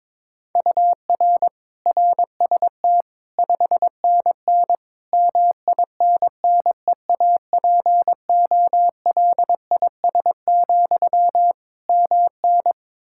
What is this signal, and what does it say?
Morse code, heard as a keyed tone, about 22 wpm, 700 Hz. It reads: UR RST 5NN MINNEAPOLIS, MN
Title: Practice QSO - Exchange 1 - Beginning - 22wpm